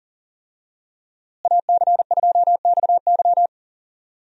Morse code, heard as a keyed tone, about 37 wpm, 700 Hz. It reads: AC2XY
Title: USA Call Signs - 5 Characters - Repeated 2x - 40wpm